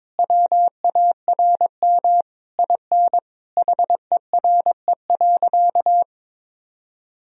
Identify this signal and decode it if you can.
Morse code, heard as a keyed tone, about 22 wpm, 700 Hz. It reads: WARM IN HERE.
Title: Sentences from Top 1000 Words - Review Part 4 - 22wpm